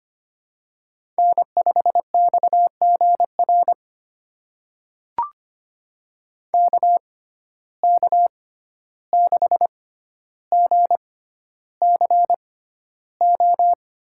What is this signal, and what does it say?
Morse code, heard as a keyed tone, about 25 wpm, spaced as for 12 wpm, 700 Hz. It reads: N5XGR KK6GCO
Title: USA Call Signs - 6x Character Spacing - 25wpm